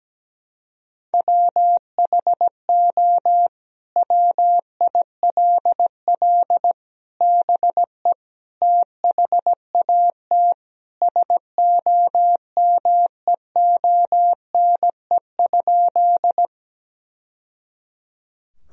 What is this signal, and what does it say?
Morse code, heard as a keyed tone, about 17 wpm, 700 Hz. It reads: WHO WILL BE THAT SOMEONE?
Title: Sentences from Top 1000 Words - Review Part 4 - 17wpm